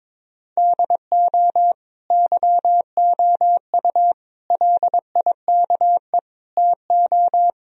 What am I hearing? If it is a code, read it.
Morse code, heard as a keyed tone, about 22 wpm, 700 Hz. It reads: DO YOU LIKE TO
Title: Sentences from Top 1000 Words - Review Part 1 - 22wpm